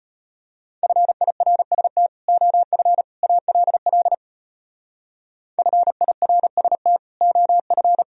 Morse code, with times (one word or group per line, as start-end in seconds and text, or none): 0.8-2.1 FIRST
2.3-3.0 OF
3.2-4.1 ALL
5.6-7.0 FIRST
7.2-8.0 OF